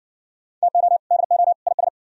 Morse code, heard as a keyed tone, about 58 wpm, 700 Hz. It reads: TO DO IS